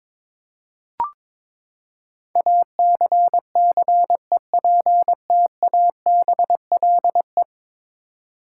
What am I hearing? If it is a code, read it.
Morse code, heard as a keyed tone, about 22 wpm, 700 Hz. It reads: ACCEPTABLE
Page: morse-code-ninja-practice-22wpm